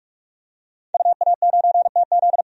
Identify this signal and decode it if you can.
Morse code, heard as a keyed tone, about 42 wpm, 700 Hz. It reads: UA9TZ
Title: DXCC-100 Call Signs - 5 Characters - 45wpm